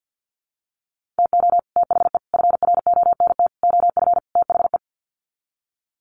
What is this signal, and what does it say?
Morse code, heard as a keyed tone, about 48 wpm, 700 Hz. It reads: TO THE FRONT OF THE
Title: Sets of 5 Words 50wpm